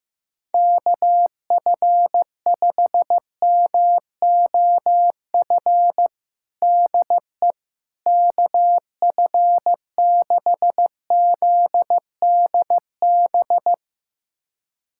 Morse code, heard as a keyed tone, about 15 wpm, 700 Hz. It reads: KF5MOF DE KF6ZDB
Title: USA Callsign DE Callsign - 6 Characters - 15wpm